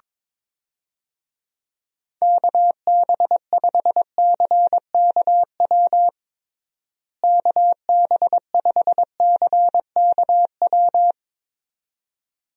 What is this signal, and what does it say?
Morse code, heard as a keyed tone, about 22 wpm, 700 Hz. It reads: KB5CKW KB5CKW
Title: USA Call Signs - 6 Characters - Repeated 2x - 22wpm